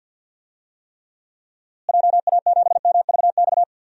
Morse code, heard as a keyed tone, about 49 wpm, 700 Hz. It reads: JA7MVX
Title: DXCC-100 Call Signs - 6 Characters - Repeated 3x - 50wpm